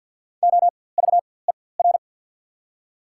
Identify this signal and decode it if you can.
Morse code, heard as a keyed tone, about 54 wpm, spaced as for 24 wpm, 700 Hz. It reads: OVER
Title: 4x Character Spacing - Top 500 Words - 50wpm